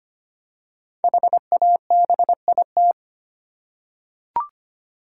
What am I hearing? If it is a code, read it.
Morse code, heard as a keyed tone, about 25 wpm, 700 Hz. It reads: HABIT
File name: 5 Letter Words - First 3 Letters Given - 25wpm